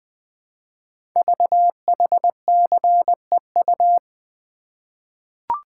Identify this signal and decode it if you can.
Morse code, heard as a keyed tone, about 20 wpm, 700 Hz. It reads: VHCEU
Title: Code Groups - 5 Letters - 20wpm